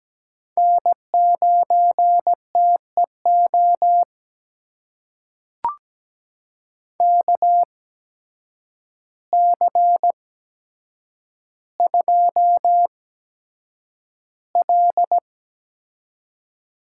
Morse code, heard as a keyed tone, about 17 wpm, spaced as for 8 wpm, 700 Hz. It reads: N9TEO KC2L
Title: USA Call Signs - 8x Character Spacing - 17wpm